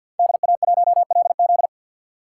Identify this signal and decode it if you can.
Morse code, heard as a keyed tone, about 46 wpm, 700 Hz. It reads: DA1LZ
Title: USA Call Signs - 3x Character Spacing - 50wpm